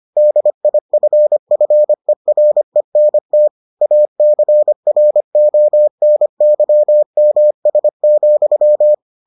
Morse code, heard as a keyed tone, about 25 wpm, 600 Hz. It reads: DIFFERENT ACRONYMS,